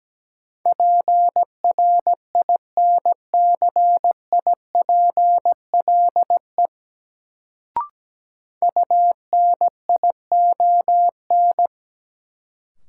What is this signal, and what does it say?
Morse code, heard as a keyed tone, about 17 wpm, 700 Hz. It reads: PRINCIPLE UNION
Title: Sentences from Top 1400 Words - New Word Drill - 17wpm